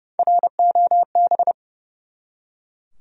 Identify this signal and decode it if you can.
Morse code, heard as a keyed tone, about 30 wpm, 700 Hz. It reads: ROB
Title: USA Common Names - Max 3 Letters - 30wpm